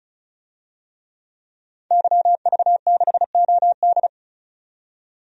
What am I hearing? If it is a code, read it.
Morse code, heard as a keyed tone, about 34 wpm, 700 Hz. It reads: YV6OD
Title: DXCC-100 Call Signs - 5 Characters - 35wpm